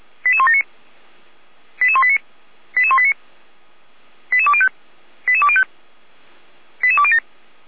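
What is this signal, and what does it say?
Five-tone selective call (ZVEI), heard as quick runs of five tones, tones of about 70 ms. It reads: 88179 88189 88189 88205 88205 88207
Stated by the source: ZVEI five-tone selective call received on an SDR.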